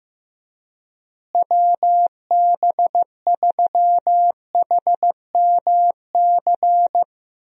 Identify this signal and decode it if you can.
Morse code, heard as a keyed tone, about 16 wpm, 700 Hz. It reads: WB3HMC
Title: USA Call Signs - 6 Characters - Repeated 2x - 15wpm